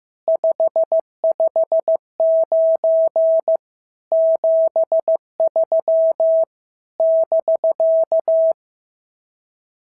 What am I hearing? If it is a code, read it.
Morse code, heard as a keyed tone, about 15 wpm, 650 Hz. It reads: 559 73 <BK>
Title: SOTA QSO Practice 15wpm